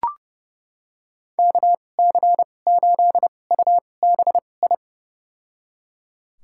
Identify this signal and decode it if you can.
Morse code, heard as a keyed tone, about 32 wpm, spaced as for 22 wpm, 700 Hz. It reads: KC8UBI
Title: USA Call Signs - 2x Character Spacing - 30wpm